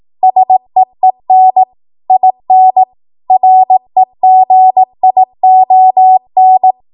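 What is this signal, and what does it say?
Morse code, heard as a keyed tone, about 18 wpm, 750 Hz. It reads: SEEN IN REGION
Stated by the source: Code practice at 18 WPM, a transition run bridging 15 to 20 WPM.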